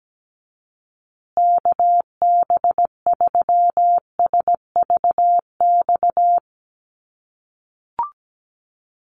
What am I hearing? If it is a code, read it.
Morse code, heard as a keyed tone, about 17 wpm, 700 Hz. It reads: KB3SVX
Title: USA Call Signs Repeated 2x 17wpm